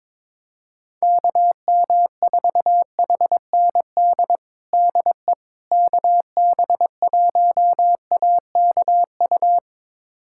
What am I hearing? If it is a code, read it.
Morse code, heard as a keyed tone, about 21 wpm, 700 Hz. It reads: KM4HND DE KB1AKU
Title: USA Callsign DE Callsign 22wpm